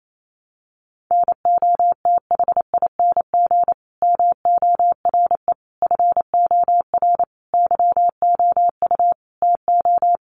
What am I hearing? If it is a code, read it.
Morse code, heard as a keyed tone, about 28 wpm, 700 Hz. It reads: NOTHING MORE FOR YOU TO
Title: Sentences from Top 1100 Words 28wpm